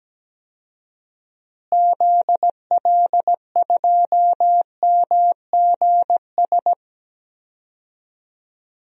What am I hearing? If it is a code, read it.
Morse code, heard as a keyed tone, about 17 wpm, 700 Hz. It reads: ZL2MGS
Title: DXCC-100 Call Signs - 6 Characters - Repeated 2x - 17wpm